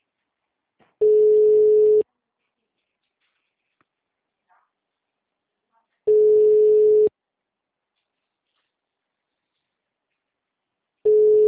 Ringback tone (Europe, 425 Hz).
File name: out- -2014- - - .141